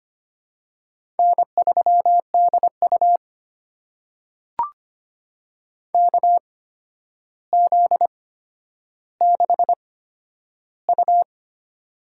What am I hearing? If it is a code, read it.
Morse code, heard as a keyed tone, about 25 wpm, spaced as for 11 wpm, 700 Hz. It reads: N3DU KZ6U
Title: USA Call Signs - 4 Characters - 8x Character Spacing - 25wpm